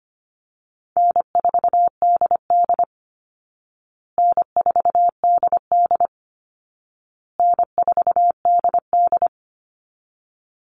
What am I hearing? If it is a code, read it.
Morse code, heard as a keyed tone, about 25 wpm, 700 Hz. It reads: N4DD N4DD N4DD
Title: Call Signs Repeated 2x - 25wpm